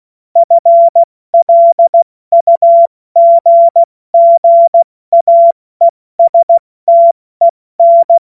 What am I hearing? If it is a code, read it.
Morse code, heard as a keyed tone, about 16 wpm, 680 Hz. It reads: FLUGGAESTEN
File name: Morsekurs / Witze / Witz 16 Januar 2024 70 F80